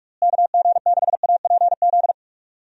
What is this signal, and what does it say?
Morse code, heard as a keyed tone, about 46 wpm, 700 Hz. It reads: KG6APZ
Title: USA Call Signs - 6 Characters - Repeated 3x - 45wpm